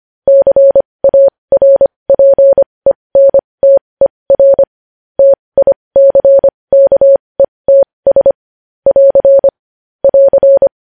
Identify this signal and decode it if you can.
Morse code, heard as a keyed tone, about 25 wpm, spaced as for 20 wpm, 550 Hz. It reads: CARPENTER TICKETS <AR> <AR>
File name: podcast / News headlines-1x-25x20